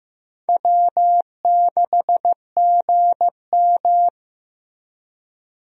Morse code, heard as a keyed tone, about 15 wpm, 700 Hz. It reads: W6GM
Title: USA Call Signs - 4 Characters - Repeated 2x - 15wpm